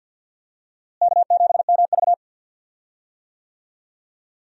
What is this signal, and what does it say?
Morse code, heard as a keyed tone, about 50 wpm, 700 Hz. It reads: K7MV
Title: USA Call Signs - 4 Characters - Repeated 2x - 50wpm